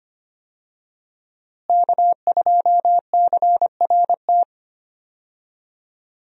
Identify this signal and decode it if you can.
Morse code, heard as a keyed tone, about 25 wpm, 700 Hz. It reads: K2CRT
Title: USA Call Signs - 5 Characters - Repeated 3x - 25wpm